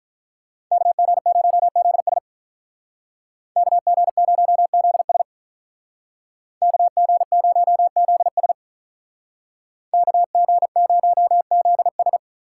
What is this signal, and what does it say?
Morse code, heard as a keyed tone, about 43 wpm, 700 Hz. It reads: KG0ZS KG0ZS KG0ZS KG0ZS
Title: Speed Racing - USA Call Signs 35wpm